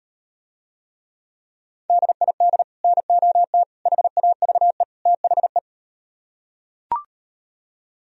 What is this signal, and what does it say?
Morse code, heard as a keyed tone, about 38 wpm, 700 Hz. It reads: DID NOT HAVE THE